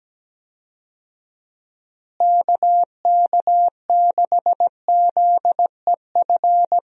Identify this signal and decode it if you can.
Morse code, heard as a keyed tone, about 17 wpm, 700 Hz. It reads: KK6ZEF
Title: USA Call Signs - 6 Characters - Repeated 3x - 17wpm